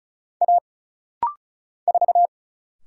Morse code, heard as a keyed tone, about 34 wpm, 700 Hz. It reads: A 4
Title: Single Letter-Number Alt - Rapid-Fire 35wpm